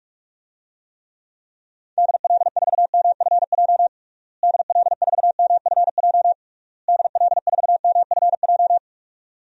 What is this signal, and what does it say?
Morse code, heard as a keyed tone, about 46 wpm, 700 Hz. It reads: DL4MFJ DL4MFJ DL4MFJ
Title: DXCC-100 Call Signs - 6 Characters - Repeated 3x - 45wpm